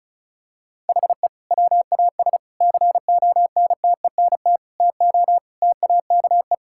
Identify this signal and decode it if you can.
Morse code, heard as a keyed tone, about 36 wpm, 700 Hz. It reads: HE WAS CONTENT TO TAKE
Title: Sentences from Top 1000 Words - Review Part 6 - 35wpm